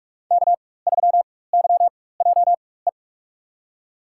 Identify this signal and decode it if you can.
Morse code, heard as a keyed tone, about 41 wpm, spaced as for 20 wpm, 700 Hz. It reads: K3YJE
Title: USA Call Signs - 4x Character Spacing - 45wpm